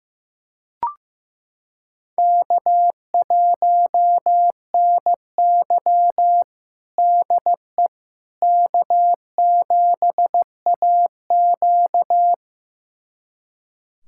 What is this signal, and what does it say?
Morse code, heard as a keyed tone, about 15 wpm, 700 Hz. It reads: K1NY DE K7AQ
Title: USA Callsign DE Callsign - 4 Characters - 15wpm